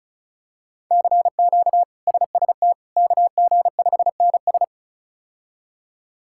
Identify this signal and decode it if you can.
Morse code, heard as a keyed tone, about 36 wpm, 700 Hz. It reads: CQ SST KG5NS
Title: K1USN SST QSO Practice - 35wpm